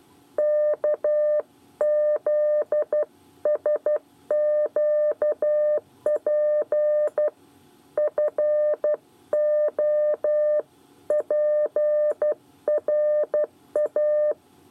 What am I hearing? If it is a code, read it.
Morse code, heard as a keyed tone, about 12 wpm, 600 Hz. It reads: KZSQP FOPRA
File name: 8 bis lezione cw 1